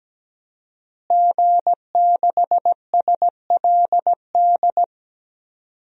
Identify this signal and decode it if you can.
Morse code, heard as a keyed tone, about 17 wpm, 700 Hz. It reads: G6SLD
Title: DXCC-100 Call Signs - 5 Characters - Repeated 3x - 17wpm